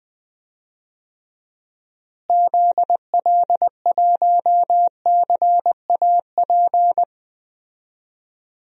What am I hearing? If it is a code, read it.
Morse code, heard as a keyed tone, about 20 wpm, 700 Hz. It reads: ZL1CAP
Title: DXCC-100 Call Signs - 6 Characters - Repeated 2x - 20wpm